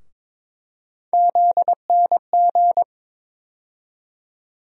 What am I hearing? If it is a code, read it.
Morse code, heard as a keyed tone, about 22 wpm, 700 Hz. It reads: ZNG
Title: USA Call Sign Postfixes - 3 Letters 22wpm